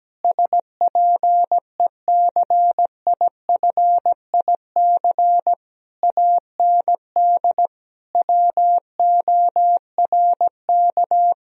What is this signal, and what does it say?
Morse code, heard as a keyed tone, about 17 wpm, 700 Hz. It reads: SPECIFIC AND WORK
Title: Sentences from Top 1000 Words - Review Part 3 - 17wpm